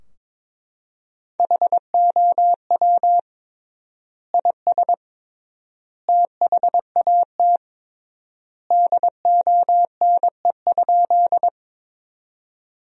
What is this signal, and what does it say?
Morse code, heard as a keyed tone, about 21 wpm, 700 Hz. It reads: HOW IS THAT DONE?
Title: 3x Word-Spacing - Sentences from Top 500 Words - 22wpm